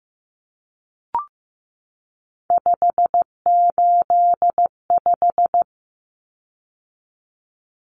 Morse code, heard as a keyed tone, about 15 wpm, 700 Hz. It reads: 585
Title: Common Cut Numbers - 3-Digit 15wpm